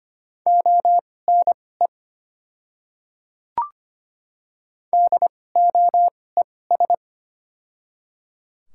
Morse code, heard as a keyed tone, about 25 wpm, spaced as for 18 wpm, 700 Hz. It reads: ONE DOES
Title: 2x Character Spacing - Top 500 Words - 25wpm